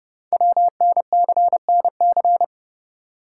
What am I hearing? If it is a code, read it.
Morse code, heard as a keyed tone, about 30 wpm, 700 Hz. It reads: WNCNC
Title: Code Groups - 5 Letters - 30wpm